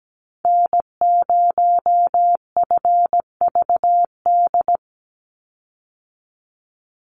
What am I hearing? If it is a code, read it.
Morse code, heard as a keyed tone, about 17 wpm, 700 Hz. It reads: N0FVD
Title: USA Call Signs 17wpm